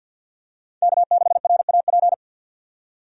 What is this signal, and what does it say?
Morse code, heard as a keyed tone, about 52 wpm, 700 Hz. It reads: K6RAP
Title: USA Call Signs - 5 Characters - 50wpm